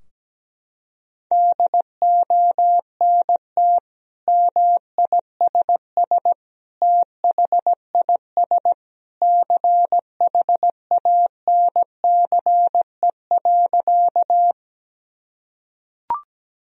Morse code, heard as a keyed tone, about 17 wpm, 700 Hz. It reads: DONT MISS THIS CHANCE.
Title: Sentences from Top 1000 Words - Review Part 3 - 17wpm